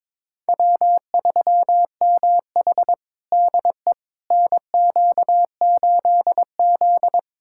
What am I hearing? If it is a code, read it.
Morse code, heard as a keyed tone, about 22 wpm, 700 Hz. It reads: W3MH DE NQ8Z
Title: USA Callsign DE Callsign - 4 Characters - 22wpm